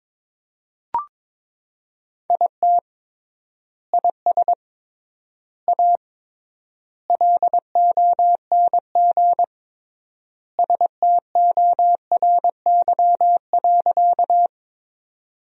Morse code, heard as a keyed tone, about 22 wpm, 700 Hz. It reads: IT IS A LONG STORY.
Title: 3x Word-Spacing - Sentences from Top 300 Words - 22wpm